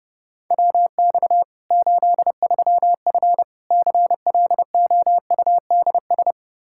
Morse code, heard as a keyed tone, about 32 wpm, 700 Hz. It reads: WX 83F CLOUDS
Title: Practice QSO - Exchange 2 - Info - 30wpm